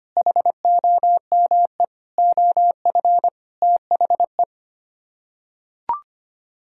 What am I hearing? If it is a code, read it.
Morse code, heard as a keyed tone, about 25 wpm, 700 Hz. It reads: HOME OF THE
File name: Sets of 9 Words - 25wpm